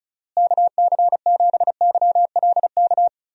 Morse code, heard as a keyed tone, about 36 wpm, 700 Hz. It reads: KC7YLK
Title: USA Call Signs - 6 Characters - Repeated 2x - 35wpm